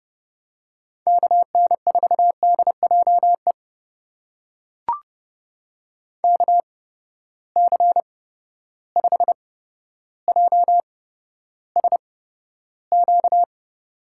Morse code, heard as a keyed tone, about 30 wpm, spaced as for 13 wpm, 700 Hz. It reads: KN4DJE KC5JSQ
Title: USA Call Signs - 6 Characters - 8x Character Spacing - 30wpm